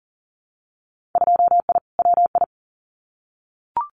Morse code, heard as a keyed tone, about 40 wpm, 700 Hz. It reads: 2I WI
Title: Winter Field Day Exchanges 40wpm